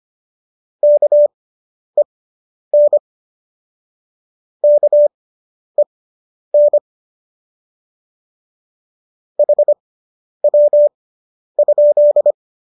Morse code, heard as a keyed tone, about 25 wpm, spaced as for 10 wpm, 590 Hz. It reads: KEN KEN HW?